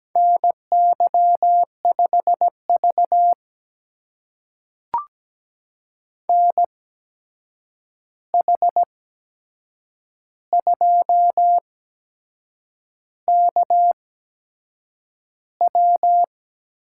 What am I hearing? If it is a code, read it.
Morse code, heard as a keyed tone, about 17 wpm, spaced as for 7 wpm, 700 Hz. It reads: NY5V NH2KW
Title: USA Call Signs - 8x Character Spacing - 17wpm